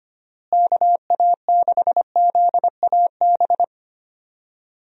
Morse code, heard as a keyed tone, about 25 wpm, 700 Hz. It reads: KA6ZAB
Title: USA Call Signs - 6 Characters - Repeated 2x - 25wpm